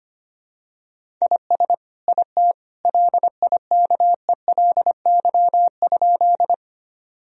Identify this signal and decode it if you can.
Morse code, heard as a keyed tone, about 25 wpm, 700 Hz. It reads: IS IT LIKELY?
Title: Sentences from Top 1000 Words - Review Part 4 - 25wpm